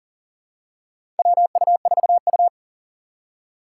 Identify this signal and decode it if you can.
Morse code, heard as a keyed tone, about 40 wpm, 700 Hz. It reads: WU4U
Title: USA Call Signs - 4 Characters - 3x Character Spacing - 40wpm